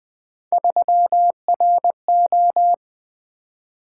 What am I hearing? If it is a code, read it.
Morse code, heard as a keyed tone, about 20 wpm, 700 Hz. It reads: 3RO
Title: Three Letter-Number Combinations - 20wpm